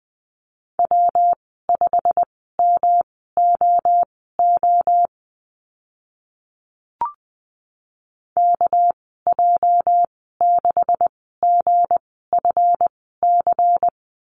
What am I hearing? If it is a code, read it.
Morse code, heard as a keyed tone, about 20 wpm, spaced as for 15 wpm, 700 Hz. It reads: W5MOO KJ6GFC
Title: 219 Lesson 35g - Number 0 - Call Signs 20wpm